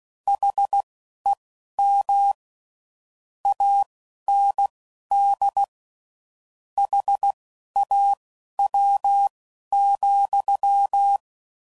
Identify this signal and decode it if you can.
Morse code, heard as a keyed tone, about 16 wpm, spaced as for 11 wpm, 800 Hz. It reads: HEM AND HAW,